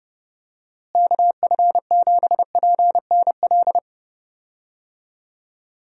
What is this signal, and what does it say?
Morse code, heard as a keyed tone, about 32 wpm, 700 Hz. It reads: KF7PNL
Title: USA Call Signs - 6 Characters - Repeated 2x - 30wpm